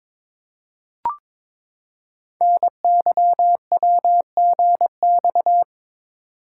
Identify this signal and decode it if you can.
Morse code, heard as a keyed tone, about 22 wpm, 700 Hz. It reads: NYWGX
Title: Code Groups - 5 Letter:Numbers - 22wpm